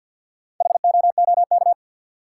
Morse code, heard as a keyed tone, about 53 wpm, 700 Hz. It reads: HOOK
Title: Sentences from Top 1600 Words - New Word Drill - 50wpm